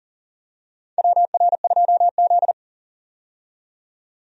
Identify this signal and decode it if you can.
Morse code, heard as a keyed tone, about 40 wpm, 700 Hz. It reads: WR2Z
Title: USA Call Signs - 4 Characters - 40wpm